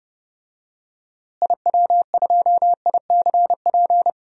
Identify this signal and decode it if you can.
Morse code, heard as a keyed tone, about 30 wpm, 700 Hz. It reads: IW2ICP
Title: DXCC-100 Call Signs - 6 Characters - Repeated 2x - 30wpm